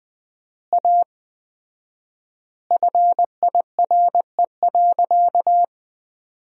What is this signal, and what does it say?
Morse code, heard as a keyed tone, about 20 wpm, 700 Hz. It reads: A FIRE.
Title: 4x Word-Spacing - Sentences from Top 400 Words - 20wpm